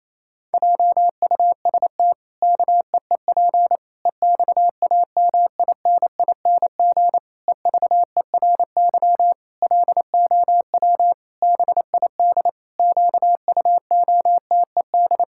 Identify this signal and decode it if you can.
Morse code, heard as a keyed tone, about 28 wpm, 700 Hz. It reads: JUST KEEP EXAMINING EVERY LOW BID QUOTED
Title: Pangram Challenge v2 - 28wpm